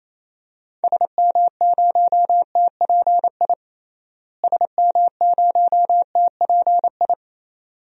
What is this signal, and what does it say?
Morse code, heard as a keyed tone, about 28 wpm, 700 Hz. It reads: SM0TPI SM0TPI
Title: DXCC-100 Call Signs - 6 Characters - Repeated 2x - 28wpm